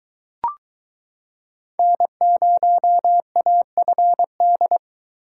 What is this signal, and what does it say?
Morse code, heard as a keyed tone, about 22 wpm, 700 Hz. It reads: N0AFD